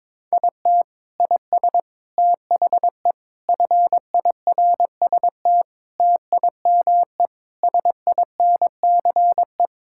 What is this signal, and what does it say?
Morse code, heard as a keyed tone, about 22 wpm, 700 Hz. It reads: IT IS THE FIRST TIME SINCE
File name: practice / 22wpm / Sentences with 4-Word Phrases - 22wpm